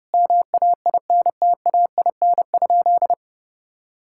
Morse code, heard as a keyed tone, about 32 wpm, 700 Hz. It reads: MAINTAIN?
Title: Sentences from Top 1000 Words - Review Part 5 - 30wpm